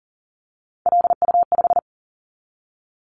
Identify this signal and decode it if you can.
Morse code, heard as a keyed tone, about 40 wpm, 700 Hz. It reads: LU5
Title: Callsign Prefixes for Top 100 DXCC - 40wpm